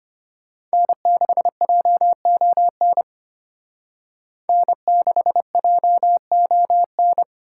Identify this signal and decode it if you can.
Morse code, heard as a keyed tone, about 32 wpm, 700 Hz. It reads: N6JON N6JON